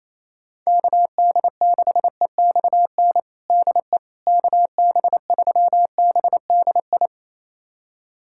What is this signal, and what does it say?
Morse code, heard as a keyed tone, about 28 wpm, 700 Hz. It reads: KD6EXN DE KB3BDI
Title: USA Callsign DE Callsign 28wpm